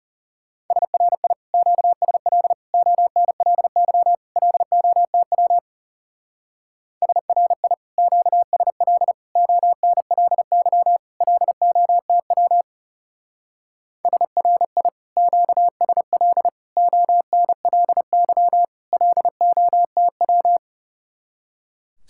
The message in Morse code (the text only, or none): SRI QSL ONLY LOTW SRI QSL ONLY LOTW SRI QSL ONLY LOTW